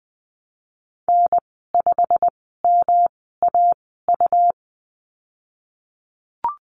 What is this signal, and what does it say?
Morse code, heard as a keyed tone, about 20 wpm, spaced as for 14 wpm, 700 Hz. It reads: N5MAU
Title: 147 Lesson 24g - Letter y - Call Signs 20wpm